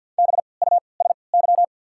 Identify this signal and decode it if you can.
Morse code, heard as a keyed tone, about 51 wpm, spaced as for 28 wpm, 700 Hz. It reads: BUSY
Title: 3x Character Spacing - Top 500 Words - 50wpm